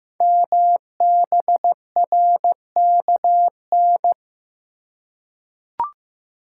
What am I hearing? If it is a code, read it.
Morse code, heard as a keyed tone, about 15 wpm, 690 Hz. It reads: MBRKN
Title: Code Groups - 5 Letters - 15wpm